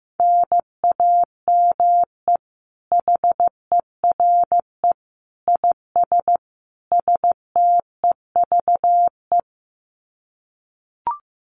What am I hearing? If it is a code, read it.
Morse code, heard as a keyed tone, about 15 wpm, 700 Hz. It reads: NAME HERE IS STEVE
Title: Common QSO Phrases 15wpm